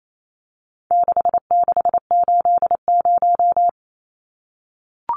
Morse code, heard as a keyed tone, about 28 wpm, 700 Hz. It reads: 6680
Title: Numbers - 4-Digit 28wpm